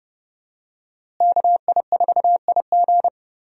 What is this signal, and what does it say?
Morse code, heard as a keyed tone, about 30 wpm, 700 Hz. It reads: KI4IG
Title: USA Call Signs - 5 Characters - Repeated 3x - 30wpm